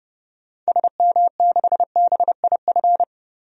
Morse code, heard as a keyed tone, about 30 wpm, 700 Hz. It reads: SM6BIF
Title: DXCC-100 Call Signs - 6 Characters - 30wpm